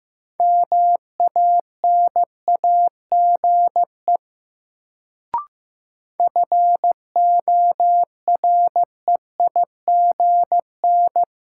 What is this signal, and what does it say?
Morse code, heard as a keyed tone, about 15 wpm, 700 Hz. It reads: MANAGE FOREIGN
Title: Sentences from Top 1400 Words - New Word Drill Alt - 15wpm